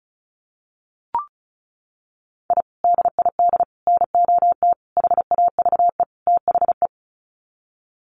Morse code, heard as a keyed tone, about 34 wpm, 700 Hz. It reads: I DID NOT HAVE THE
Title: Sets of 5 Words Encore 35wpm